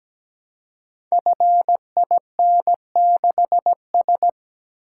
Morse code, heard as a keyed tone, about 17 wpm, 700 Hz. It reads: FIN6S
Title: Code Groups - 5 Letter:Numbers - 17wpm